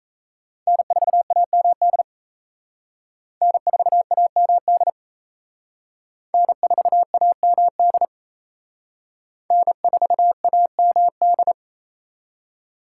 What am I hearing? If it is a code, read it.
Morse code, heard as a keyed tone, about 36 wpm, 700 Hz. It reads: N4AMD N4AMD N4AMD N4AMD